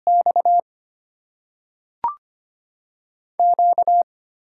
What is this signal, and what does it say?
Morse code, heard as a keyed tone, about 25 wpm, 700 Hz. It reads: X Q
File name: Top 20 QSO Elements - 6x Character Spacing - 25wpm